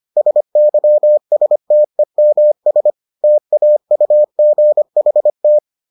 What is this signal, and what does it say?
Morse code, heard as a keyed tone, about 25 wpm, 600 Hz. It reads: SYSTEMS TAUGHT